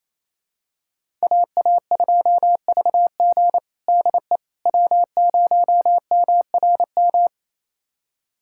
Morse code, heard as a keyed tone, about 28 wpm, 700 Hz. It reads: AA2VG DE W0MRM
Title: USA Callsign DE Callsign - 5 Characters - 28wpm